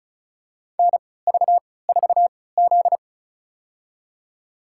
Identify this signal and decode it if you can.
Morse code, heard as a keyed tone, about 36 wpm, spaced as for 20 wpm, 700 Hz. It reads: NV4Z
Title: USA Call Signs - 4 Characters - 3x Character Spacing - 35wpm